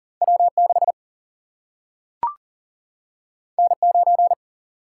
Morse code, heard as a keyed tone, about 38 wpm, 700 Hz. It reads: W 6 N 9
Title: Single-Letter Single-Number 40wpm